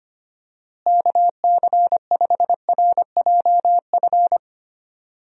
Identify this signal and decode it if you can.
Morse code, heard as a keyed tone, about 25 wpm, 700 Hz. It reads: KC5RJF
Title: USA Call Signs - 6 Characters - Repeated 2x - 25wpm